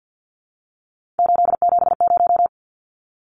Morse code, heard as a keyed tone, about 50 wpm, 700 Hz. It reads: 870
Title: Numbers - 3-Digit 50wpm